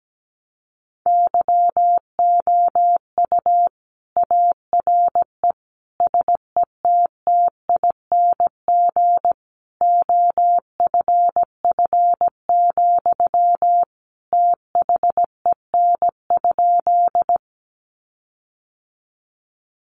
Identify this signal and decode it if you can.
Morse code, heard as a keyed tone, about 17 wpm, 700 Hz. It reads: YOU ARE SETTING OFF, THEN?
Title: Sentences from Top 1300 Words 17wpm